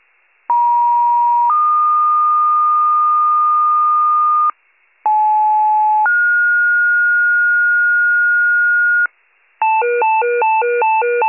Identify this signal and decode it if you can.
Two-tone page, heard as a one-second tone, then a three-second tone, about 950 then 1220 Hz, about 860 then 1430 Hz, with noise. It is heard from 0.5 to 9.1 s.